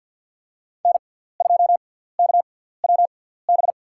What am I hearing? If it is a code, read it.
Morse code, heard as a keyed tone, about 53 wpm, spaced as for 17 wpm, 700 Hz. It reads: N2KWB
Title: USA Call Signs - 6x Character Spacing - 50wpm